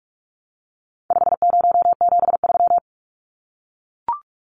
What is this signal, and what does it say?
Morse code, heard as a keyed tone, about 44 wpm, 700 Hz. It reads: 5073
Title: Numbers - 4-Digit 45wpm